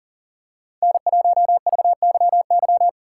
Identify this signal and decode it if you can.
Morse code, heard as a keyed tone, about 40 wpm, 700 Hz. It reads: N1VYY
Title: USA Callsign DE Callsign - 5 Characters - 40wpm